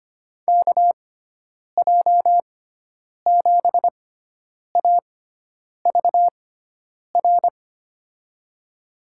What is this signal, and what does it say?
Morse code, heard as a keyed tone, about 25 wpm, spaced as for 9 wpm, 700 Hz. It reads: KJ7AVR
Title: USA Call Signs - 6x Character Spacing - 25wpm